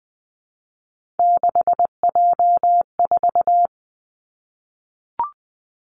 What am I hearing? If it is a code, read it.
Morse code, heard as a keyed tone, about 20 wpm, 700 Hz. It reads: 6J4
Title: Three Challenging Letter-Number Combinations 20wpm